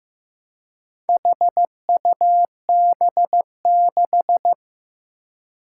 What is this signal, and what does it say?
Morse code, heard as a keyed tone, about 15 wpm, 700 Hz. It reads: HUB6
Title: Four Challenging Letter-Number Combinations 15wpm